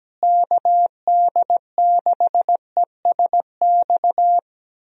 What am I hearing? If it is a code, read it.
Morse code, heard as a keyed tone, about 17 wpm, 700 Hz. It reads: KD6ESX
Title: USA Call Signs - 6 Characters - Repeated 2x - 17wpm